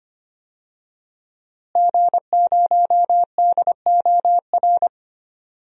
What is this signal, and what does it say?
Morse code, heard as a keyed tone, about 25 wpm, 700 Hz. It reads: G0DOR
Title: DXCC-100 Call Signs - 5 Characters - Repeated 2x - 25wpm